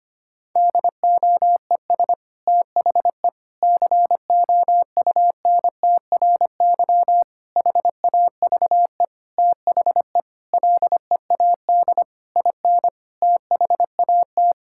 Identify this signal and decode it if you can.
Morse code, heard as a keyed tone, about 25 wpm, 700 Hz. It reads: DOES THE COUNTRY HAVE THE LEAD IN THAT
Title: Sentences from Top 1000 Words - Review Part 5 - 25wpm